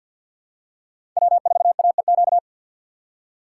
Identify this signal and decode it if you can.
Morse code, heard as a keyed tone, about 51 wpm, 690 Hz. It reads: W4AEQ
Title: USA Call Signs - 5 Characters - 4x Character Spacing - 50wpm